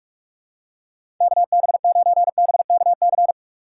Morse code, heard as a keyed tone, about 41 wpm, 700 Hz. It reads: KB9BKC
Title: USA Call Signs - 6 Characters - 4x Character Spacing - 45wpm